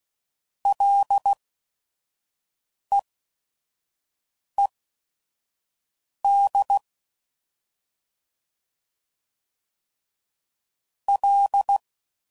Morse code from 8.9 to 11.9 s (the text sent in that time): L